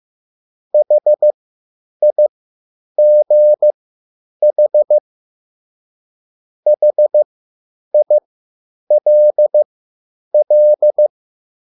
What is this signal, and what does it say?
Morse code, heard as a keyed tone, about 15 wpm, spaced as for 9 wpm, 600 Hz. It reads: HIGH HILL